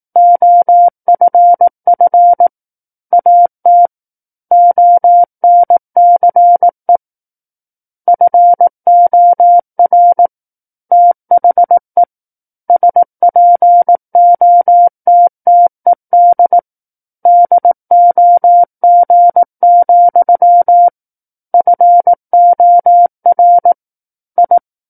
Morse code, heard as a keyed tone, about 18 wpm, 700 Hz. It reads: OFF AT ONCE FOR THE SPOTTED DOG, FOR I
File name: 18 WPM